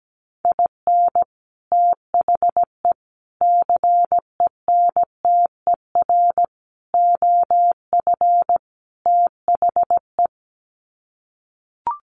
Morse code, heard as a keyed tone, about 17 wpm, 700 Hz. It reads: IN THE CENTER OF THE
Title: Sets of 5 Words 17wpm